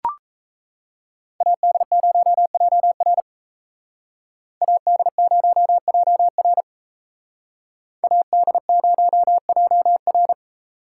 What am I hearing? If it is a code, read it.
Morse code, heard as a keyed tone, about 32 wpm, 700 Hz. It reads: AD0JR AD0JR AD0JR
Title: Speed Racing - USA 2x2 Call Signs 28wpm